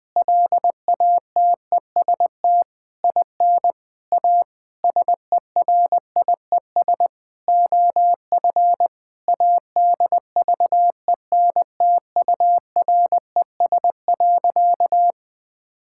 Morse code, heard as a keyed tone, about 20 wpm, 700 Hz. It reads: LATEST IN A SERIES OF ADVENTURES.
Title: Sentences with 7-Word Phrases - 20wpm